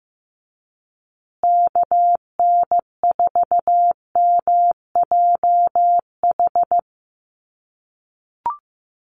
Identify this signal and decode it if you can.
Morse code, heard as a keyed tone, about 15 wpm, 700 Hz. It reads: KN4MJH
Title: USA Call Signs Repeated 2x 15wpm